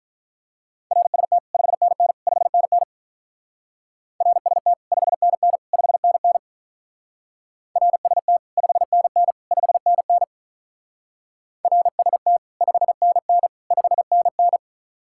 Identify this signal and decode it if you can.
Morse code, heard as a keyed tone, about 46 wpm, 700 Hz. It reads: RST 5NN 5NN RST 5NN 5NN RST 5NN 5NN RST 5NN 5NN